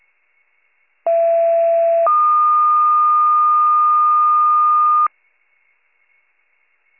Two-tone page, heard as a one-second tone, then a three-second tone, about 670 then 1220 Hz. It is heard from 1.1 to 5.1 s.